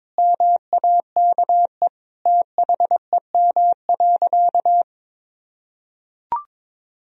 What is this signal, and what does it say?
Morse code, heard as a keyed tone, about 22 wpm, 700 Hz. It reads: MAKE THEM.
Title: Sentences from Top 1000 Words - Review Part 1 - 22wpm